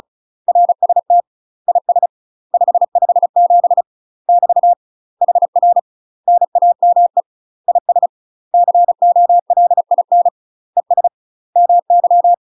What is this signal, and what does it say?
Morse code, heard as a keyed tone, about 36 wpm, 700 Hz. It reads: RST IS 557 <BT> HR NAME IS COLIN ES MY
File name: UK MASTER QSOs TRAINING / 35WPM / UK MASTER QSOs 0042